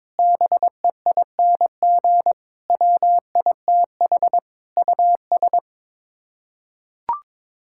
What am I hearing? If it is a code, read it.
Morse code, heard as a keyed tone, about 22 wpm, 700 Hz. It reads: BEING WITH US
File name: practice / 22wpm / Sets of 8 Words - 22wpm